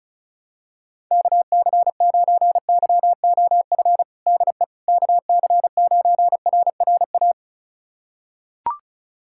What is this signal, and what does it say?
Morse code, heard as a keyed tone, about 36 wpm, 700 Hz. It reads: KC9YOF DE KC9RRA
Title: USA Callsign DE Callsign 35wpm